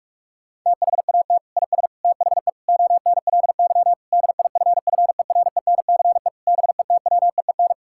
Morse code, heard as a keyed tone, about 46 wpm, 700 Hz. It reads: THAT IS THE ONLY DIFFERENCE BETWEEN
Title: Sentences from Top 1000 Words - Review Part 5 - 45wpm